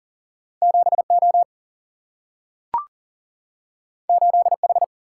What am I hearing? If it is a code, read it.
Morse code, heard as a keyed tone, about 39 wpm, 700 Hz. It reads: 7O 8H
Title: Single-Letter Single-Number 40wpm